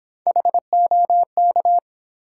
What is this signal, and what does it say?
Morse code, heard as a keyed tone, about 28 wpm, 700 Hz. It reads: HOK